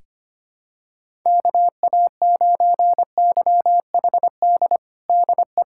Morse code, heard as a keyed tone, about 25 wpm, 700 Hz. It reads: KA9YHD DE
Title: USA Callsign DE Callsign 25wpm